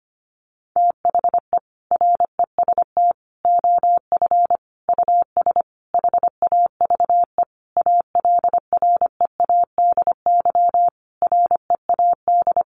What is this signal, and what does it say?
Morse code, heard as a keyed tone, about 25 wpm, 700 Hz. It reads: THE REST OF US HAVE ALREADY READ
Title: Sentences from Top 700 Words 25wpm